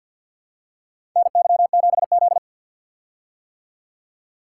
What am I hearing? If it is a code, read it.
Morse code, heard as a keyed tone, about 50 wpm, 700 Hz. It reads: NY7Z
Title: K1USN SST QSO Practice - 50wpm